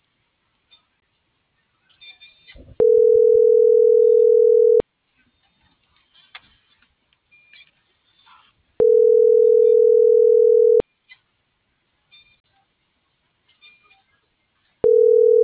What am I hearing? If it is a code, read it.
Ringback tone (North America), 440 + 480 Hz.